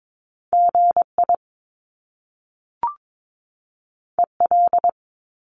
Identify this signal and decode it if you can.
Morse code, heard as a keyed tone, about 22 wpm, 700 Hz. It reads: GI EL
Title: Top Two Letter Combinations in Words 22wpm